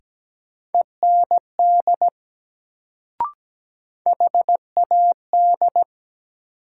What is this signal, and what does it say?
Morse code, heard as a keyed tone, about 17 wpm, 700 Hz. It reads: END HAD
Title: Sets of 2 Words - New Word Drill - 17wpm